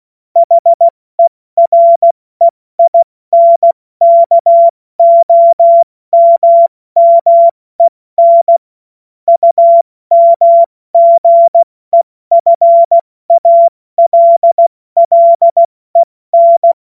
Morse code, heard as a keyed tone, about 16 wpm, 680 Hz. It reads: HEREINKOMMEN UMGEFALLEN
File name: Morsekurs / Witze / Witz 04. Juli 2023 70 F80